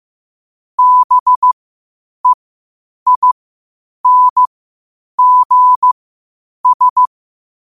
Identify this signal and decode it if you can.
Morse code, heard as a keyed tone, about 15 wpm, spaced as for 8 wpm, 1000 Hz. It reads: BEINGS
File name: qotdfiles / 8wpm